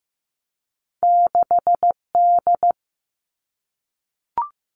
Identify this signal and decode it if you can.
Morse code, heard as a keyed tone, about 15 wpm, 700 Hz. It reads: 6D
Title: Two Challenging Letter-Number Combinations 15wpm